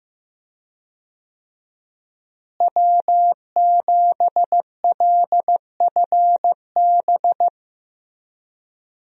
Morse code, heard as a keyed tone, about 15 wpm, 700 Hz. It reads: W7LFB
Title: USA Call Signs - 5 Characters - Repeated 3x - 15wpm